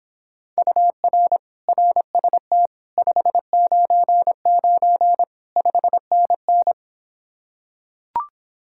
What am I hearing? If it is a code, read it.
Morse code, heard as a keyed tone, about 26 wpm, 700 Hz. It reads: UR RST 599 5NN